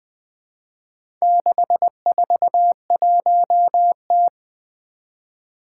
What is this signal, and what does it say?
Morse code, heard as a keyed tone, about 20 wpm, 700 Hz. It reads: 641T
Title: Four Challenging Letter-Number Combinations 20wpm